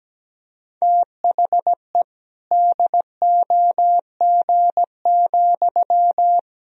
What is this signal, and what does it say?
Morse code, heard as a keyed tone, about 17 wpm, 700 Hz. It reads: THE DOG,
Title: Sentences from Top 1000 Words - Review Part 3 - 17wpm